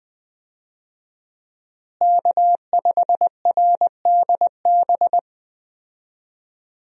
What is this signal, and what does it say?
Morse code, heard as a keyed tone, about 20 wpm, 700 Hz. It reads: K5RDB
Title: USA Call Signs - 5 Characters - Repeated 3x - 20wpm